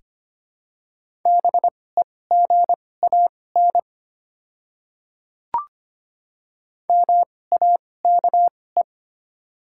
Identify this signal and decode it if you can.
Morse code, heard as a keyed tone, about 25 wpm, spaced as for 18 wpm, 700 Hz. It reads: BEGAN MAKE
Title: 2x Character Spacing - Top 500 Words - 25wpm